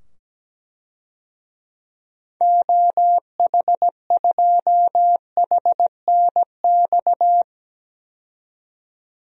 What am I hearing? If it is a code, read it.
Morse code, heard as a keyed tone, about 17 wpm, 700 Hz. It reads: OH2HNX
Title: DXCC-100 Call Signs - 6 Characters - 17wpm